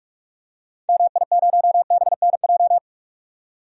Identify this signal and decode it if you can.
Morse code, heard as a keyed tone, about 47 wpm, 700 Hz. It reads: MI0BNJ
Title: DXCC-100 Call Signs - 6 Characters - 45wpm